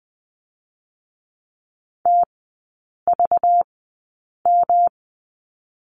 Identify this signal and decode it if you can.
Morse code, heard as a keyed tone, about 20 wpm, spaced as for 8 wpm, 690 Hz. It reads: TVM
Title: 205 Lesson 33e - Letter j - Three Characters 20wpm